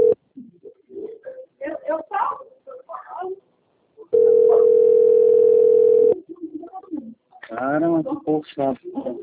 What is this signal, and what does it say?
Ringback tone (North America), 440 + 480 Hz, with speech.
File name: rg-1600-unknown- - - .3141